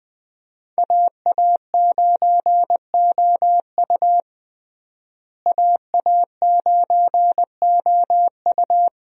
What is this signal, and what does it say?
Morse code, heard as a keyed tone, about 21 wpm, 700 Hz. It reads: AA9OU AA9OU
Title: USA Call Signs - 5 Characters - Repeated 3x - 20wpm